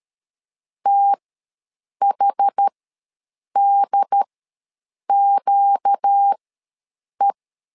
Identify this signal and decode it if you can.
Morse code, heard as a keyed tone, about 13 wpm, spaced as for 7 wpm, 800 Hz. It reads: THDQE
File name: LEZIONI / 4 Q ws 12wpm 1912